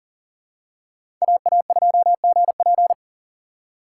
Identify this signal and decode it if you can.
Morse code, heard as a keyed tone, about 40 wpm, 700 Hz. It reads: AA2GP
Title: USA Call Signs - 5 Characters - 4x Character Spacing - 40wpm